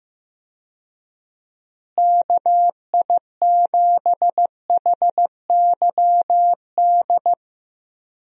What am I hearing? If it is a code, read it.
Morse code, heard as a keyed tone, about 14 wpm, 700 Hz. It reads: KI7HYD
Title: USA Call Signs - 6 Characters - Repeated 3x - 15wpm